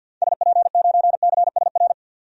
Morse code, heard as a keyed tone, about 51 wpm, 700 Hz. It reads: SP9CSR
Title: DXCC-100 Call Signs - 6 Characters - 50wpm